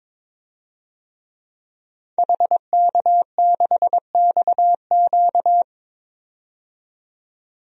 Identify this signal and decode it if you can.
Morse code, heard as a keyed tone, about 22 wpm, 700 Hz. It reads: HK6XQ
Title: DXCC-100 Call Signs - 5 Characters - 22wpm